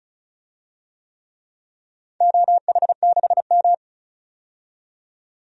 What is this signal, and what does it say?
Morse code, heard as a keyed tone, about 34 wpm, 700 Hz. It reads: OH6M
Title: DXCC-100 Call Signs - 4 Characters - 35wpm